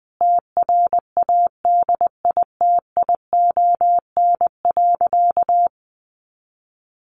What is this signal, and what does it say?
Morse code, heard as a keyed tone, about 20 wpm, 700 Hz. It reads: TRADITION.
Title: Sentences from Top 1500 Words 20wpm